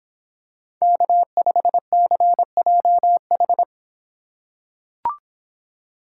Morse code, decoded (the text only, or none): K5CJH